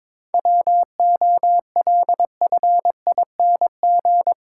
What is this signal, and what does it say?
Morse code, heard as a keyed tone, about 22 wpm, 700 Hz. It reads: WOLFING
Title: Pangram Challenge v2 - 22wpm